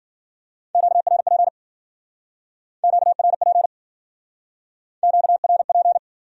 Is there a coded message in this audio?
Yes: Morse code from 0.8 to 6.0 s, heard as a keyed tone, about 52 wpm, 700 Hz. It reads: QRP QRP QRP